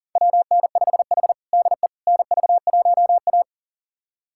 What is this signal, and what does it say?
Morse code, heard as a keyed tone, about 37 wpm, 700 Hz. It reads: WN5H DE NV1A
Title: USA Callsign DE Callsign - 4 Characters - 40wpm